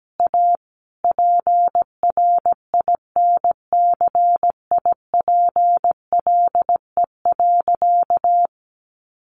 Morse code, heard as a keyed tone, about 17 wpm, 700 Hz. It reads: A PRINCIPLE.
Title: Sentences from Top 1400 Words 17wpm